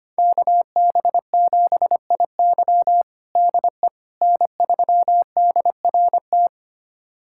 Morse code, heard as a keyed tone, about 25 wpm, 700 Hz. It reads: KB7IY DE N3DRT
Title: USA Callsign DE Callsign - 5 Characters - 25wpm